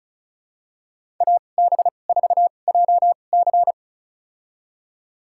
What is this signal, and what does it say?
Morse code, heard as a keyed tone, about 36 wpm, spaced as for 25 wpm, 690 Hz. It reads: AB4JC
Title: USA Call Signs - 5 Characters - 2x Character Spacing - 35wpm